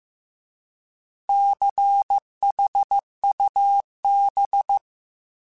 Morse code, heard as a keyed tone, about 15 wpm, 800 Hz. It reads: CHUB